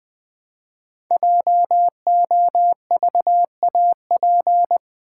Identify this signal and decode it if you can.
Morse code, heard as a keyed tone, about 20 wpm, 700 Hz. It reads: JOVAP
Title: Code Groups - 5 Letters - 20wpm